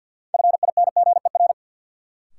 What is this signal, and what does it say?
Morse code, heard as a keyed tone, about 52 wpm, 700 Hz. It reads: FINGER
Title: Sentences from Top 1600 Words - New Word Drill - 50wpm